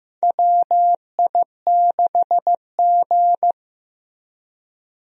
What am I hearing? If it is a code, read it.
Morse code, heard as a keyed tone, about 15 wpm, 700 Hz. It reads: WI6G
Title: USA Call Signs - 4 Characters - Repeated 3x - 15wpm